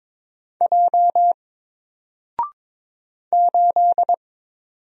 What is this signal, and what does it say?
Morse code, heard as a keyed tone, about 21 wpm, 700 Hz. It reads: J 8
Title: Single Letter-Number Alt 22wpm